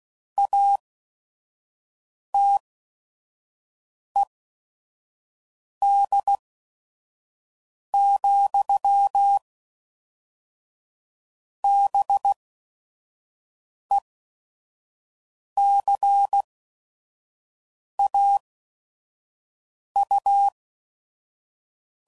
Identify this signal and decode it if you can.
Morse code, heard as a keyed tone, about 16 wpm, spaced as for 5 wpm, 800 Hz. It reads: ATED, BECAU
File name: CW Code Groups @5WPM / Technical Subjects / XC